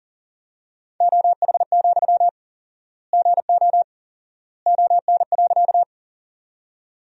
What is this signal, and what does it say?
Morse code, heard as a keyed tone, about 40 wpm, 700 Hz. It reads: OH, GO ON.
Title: 4x Word-Spacing - Sentences from Top 500 Words - 40wpm